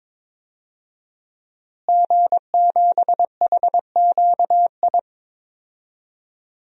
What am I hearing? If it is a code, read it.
Morse code, heard as a keyed tone, about 22 wpm, 700 Hz. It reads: G7HQI
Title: DXCC-100 Call Signs - 5 Characters - Repeated 2x - 22wpm